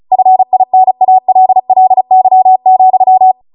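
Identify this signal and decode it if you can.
Morse code, heard as a keyed tone, about 36 wpm, 750 Hz. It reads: FINALLY,
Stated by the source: Amateur radio Morse code practice archive transmission at 35 wpm.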